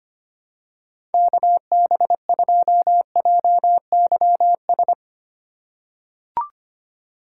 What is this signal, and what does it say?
Morse code, heard as a keyed tone, about 25 wpm, 700 Hz. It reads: KB2JYS
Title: USA Call Signs - 6 Characters - 3x Character Spacing - 25wpm